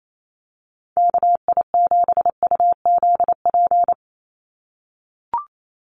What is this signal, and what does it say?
Morse code, heard as a keyed tone, about 28 wpm, 700 Hz. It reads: KI7UZP
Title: USA Call Signs 28wpm